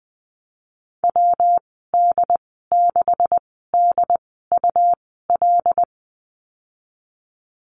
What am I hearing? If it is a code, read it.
Morse code, heard as a keyed tone, about 19 wpm, spaced as for 14 wpm, 700 Hz. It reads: WD6DUL
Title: 128 Lesson 21g - Number 6 - Call Signs 20wpm